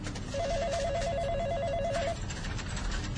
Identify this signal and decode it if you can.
There is no code.